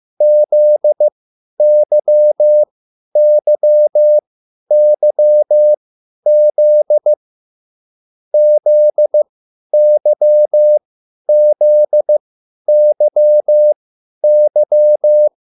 Morse code, heard as a keyed tone, about 15 wpm, spaced as for 10 wpm, 600 Hz. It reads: ZYYYZ ZYZYY